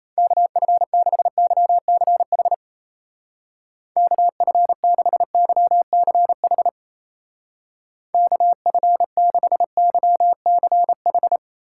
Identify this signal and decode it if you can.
Morse code, heard as a keyed tone, about 36 wpm, 700 Hz. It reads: KF6YCH KF6YCH KF6YCH